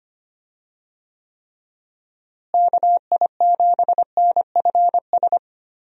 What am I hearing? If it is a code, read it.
Morse code, heard as a keyed tone, about 25 wpm, 700 Hz. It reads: KI7NFS
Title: USA Call Signs - 6 Characters - Repeated 2x - 25wpm